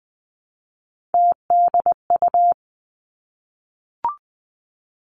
Morse code, heard as a keyed tone, about 20 wpm, 700 Hz. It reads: TDU
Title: Three Challenging Letter-Number Combinations 20wpm